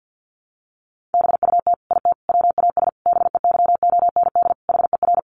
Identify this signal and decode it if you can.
Morse code, heard as a keyed tone, about 50 wpm, 700 Hz. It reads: BUT IT WAS BEYOND HER
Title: Sentences from Top 1200 Words 50wpm